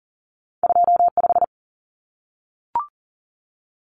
Morse code, heard as a keyed tone, about 40 wpm, 700 Hz. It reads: 25
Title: Common Cut Numbers - 2-Digit 40wpm